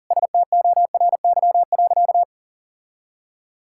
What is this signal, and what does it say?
Morse code, heard as a keyed tone, about 37 wpm, 700 Hz. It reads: STORY.
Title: Sentences from Top 1000 Words - Review Part 6 - 40wpm